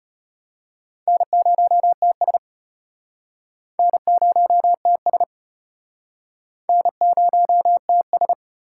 Morse code, heard as a keyed tone, about 32 wpm, 700 Hz. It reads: N0TS N0TS N0TS